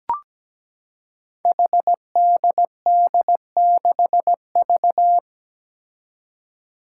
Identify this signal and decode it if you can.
Morse code, heard as a keyed tone, about 17 wpm, 700 Hz. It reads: HDD6V
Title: Five Challenging Letter-Number Combinations 17wpm